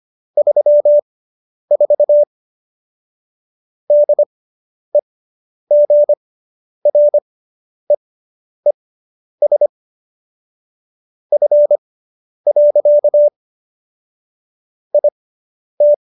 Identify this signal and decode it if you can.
Morse code, heard as a keyed tone, about 25 wpm, spaced as for 10 wpm, 590 Hz. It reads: 34 DEGREES F. IT